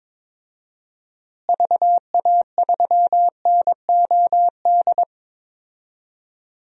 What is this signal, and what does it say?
Morse code, heard as a keyed tone, about 22 wpm, 700 Hz. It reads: VA3NOD
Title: DXCC-100 Call Signs - 6 Characters - Repeated 2x - 22wpm